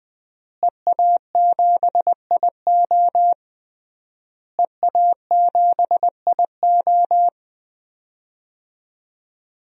Morse code, heard as a keyed tone, about 21 wpm, 700 Hz. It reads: EA7IO EA7IO
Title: DXCC-100 Call Signs - 5 Characters - Repeated 3x - 20wpm